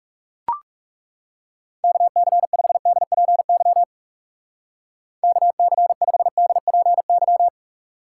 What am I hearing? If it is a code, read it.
Morse code, heard as a keyed tone, about 36 wpm, 700 Hz. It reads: KC5DPY KC5DPY
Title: Speed Racing - USA 2x3 Call Signs 30wpm